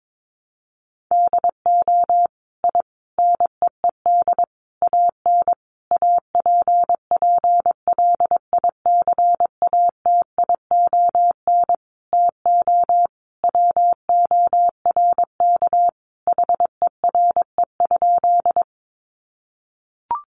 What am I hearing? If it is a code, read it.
Morse code, heard as a keyed tone, about 22 wpm, 700 Hz. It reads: DO I NEED AN APPLICATION TO WORK HERE?
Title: Sentences from Top 1200 Words 22wpm